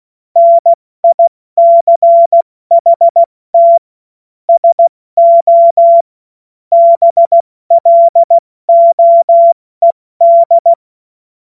Morse code, heard as a keyed tone, about 16 wpm, 680 Hz. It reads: NICHT SO BLOED
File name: Morsekurs / Witze / Witz 19 Dezember 2023 70 F80